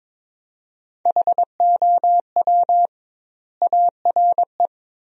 Morse code, heard as a keyed tone, about 22 wpm, 700 Hz. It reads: HOW ARE
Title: 2x Word-Spacing - Sentences from Top 100 Words - 22wpm